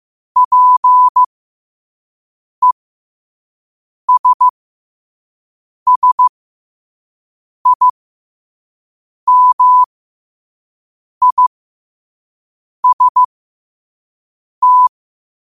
Morse code, heard as a keyed tone, about 15 wpm, spaced as for 5 wpm, 1000 Hz. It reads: PESSIMIST
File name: qotdfiles / 5wpm